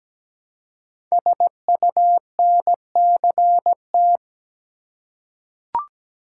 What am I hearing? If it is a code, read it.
Morse code, heard as a keyed tone, about 17 wpm, 700 Hz. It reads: SUNCT
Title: Code Groups - 5 Letter:Numbers - 17wpm